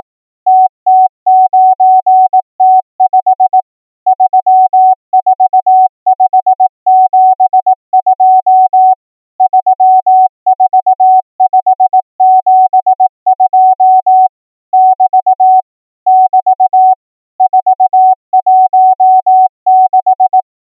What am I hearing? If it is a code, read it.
Morse code, heard as a keyed tone, about 18 wpm, 750 Hz. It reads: TT9T5 34572 34572 <BT> <BT> 416
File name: numbers / V2-morse